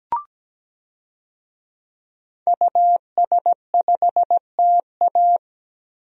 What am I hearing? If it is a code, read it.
Morse code, heard as a keyed tone, about 17 wpm, 700 Hz. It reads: US5TA
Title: DXCC-100 Call Signs - 5 Characters - 17wpm